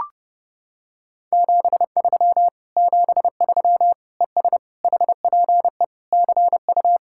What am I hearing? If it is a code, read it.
Morse code, heard as a keyed tone, about 30 wpm, 700 Hz. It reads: 73 73 ES HPE CU